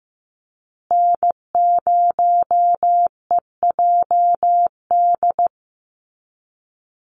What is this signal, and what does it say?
Morse code, heard as a keyed tone, about 15 wpm, 700 Hz. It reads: N0EJD
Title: USA Call Signs Repeated 2x 15wpm